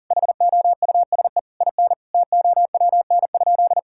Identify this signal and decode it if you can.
Morse code, heard as a keyed tone, about 40 wpm, 700 Hz. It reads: HOUSE IN TOWN?
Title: Sentences from Top 1000 Words - Review Part 6 - 40wpm